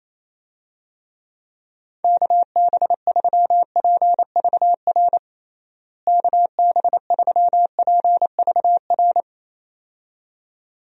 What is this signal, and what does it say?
Morse code, heard as a keyed tone, about 28 wpm, 700 Hz. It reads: KB3PVR KB3PVR
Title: USA Call Signs - 6 Characters - Repeated 2x - 28wpm